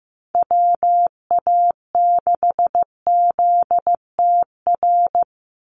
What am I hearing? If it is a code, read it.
Morse code, heard as a keyed tone, about 15 wpm, 700 Hz. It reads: WA6ZTR
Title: USA Call Signs 15wpm